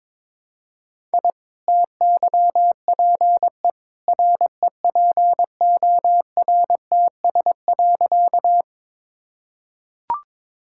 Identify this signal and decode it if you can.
Morse code, heard as a keyed tone, about 21 wpm, 700 Hz. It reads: I TYPE REPORTS.
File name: practice / 22wpm / Easy Sentences with Hints - 22wpm